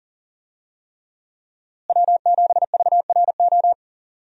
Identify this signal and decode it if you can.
Morse code, heard as a keyed tone, about 40 wpm, 700 Hz. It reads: W7VRO
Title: USA Call Signs - 5 Characters - Repeated 3x - 40wpm